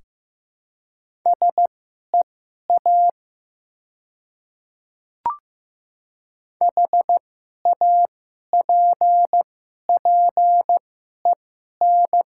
Morse code, heard as a keyed tone, about 15 wpm, spaced as for 11 wpm, 700 Hz. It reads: SEA HAPPEN
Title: 2x Character Spacing - Top 500 Words - 15wpm